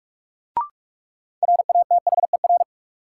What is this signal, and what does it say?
Morse code, heard as a keyed tone, about 42 wpm, 700 Hz. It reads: RATHER
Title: Sets of 2 Words - New Word Drill - 45wpm